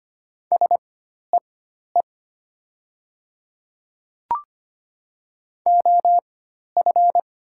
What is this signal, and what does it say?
Morse code, heard as a keyed tone, about 25 wpm, spaced as for 12 wpm, 700 Hz. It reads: SEE OF
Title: 4x Character Spacing - Top 100 Words - 25wpm